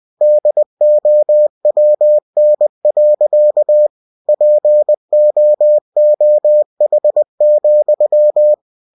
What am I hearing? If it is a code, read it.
Morse code, heard as a keyed tone, about 20 wpm, 600 Hz. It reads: DOWN. POOH,
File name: cw-books / winnie-the-pooh / winnie-20wpm-page-0087